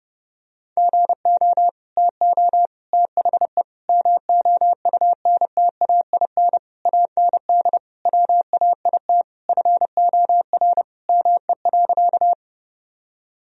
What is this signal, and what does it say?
Morse code, heard as a keyed tone, about 32 wpm, 700 Hz. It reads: GO TO THE MOUNTAIN AND WAIT FOR ME.
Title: Sentences from Top 1000 Words - Review Part 1 - 30wpm